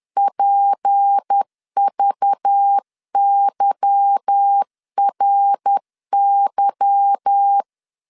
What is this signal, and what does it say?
Morse code, heard as a keyed tone, about 10 wpm, 800 Hz. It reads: PVYRY
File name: LEZIONI / 8 YJ45 / 8 Y 10wpm 2615